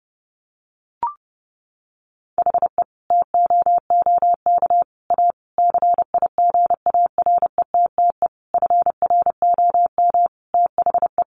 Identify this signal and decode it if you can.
Morse code, heard as a keyed tone, about 30 wpm, 700 Hz. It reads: HE TOOK A CIGARETTE FROM THE
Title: Sentences from Top 1000 Words 30wpm